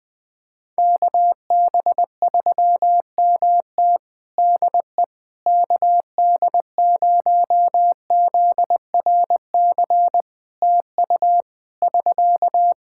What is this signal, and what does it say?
Morse code, heard as a keyed tone, about 19 wpm, 700 Hz. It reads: KB3MT DE KD0ZRC TU <SK>
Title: Practice QSO - Exchange 4 - Ending - 20wpm